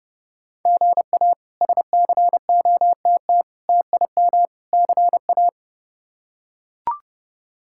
Morse code, heard as a keyed tone, about 30 wpm, 700 Hz. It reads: GA SCOTT TIM CA
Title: K1USN SST QSO Practice - 30wpm